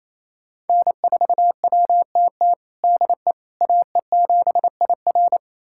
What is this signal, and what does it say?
Morse code, heard as a keyed tone, about 28 wpm, 700 Hz. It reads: N4WTT DE AE7IR
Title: USA Callsign DE Callsign - 5 Characters - 28wpm